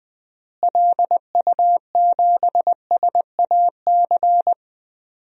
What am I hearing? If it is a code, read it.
Morse code, heard as a keyed tone, about 20 wpm, 700 Hz. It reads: LU7SAC
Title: DXCC-100 Call Signs - 6 Characters - Repeated 3x - 20wpm